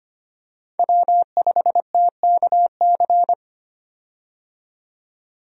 Morse code, heard as a keyed tone, about 25 wpm, 700 Hz. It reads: W5TKC
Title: USA Call Signs - 5 Characters - 25wpm